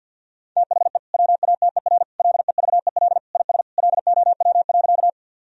Morse code, heard as a keyed tone, about 50 wpm, 700 Hz. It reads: THE WATER LEVEL IS LOW.
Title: Sentences from Top 1000 Words - Review Part 4 - 50wpm